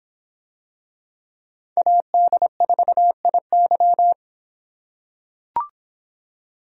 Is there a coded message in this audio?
Yes: Morse code from 1.8 to 4.1 s, heard as a keyed tone, about 26 wpm, 700 Hz. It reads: AD4IY